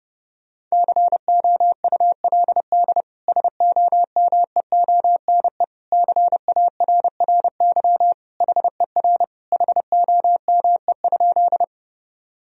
Morse code, heard as a keyed tone, about 32 wpm, 700 Hz. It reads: COULD SOMEONE CARRY HER HOME?
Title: Sentences from Top 1000 Words - Review Part 4 - 30wpm